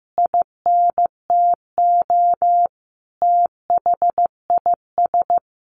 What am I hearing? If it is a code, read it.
Morse code, heard as a keyed tone, about 15 wpm, 700 Hz. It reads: INTO THIS
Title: Sentences from Top 1400 Words 15wpm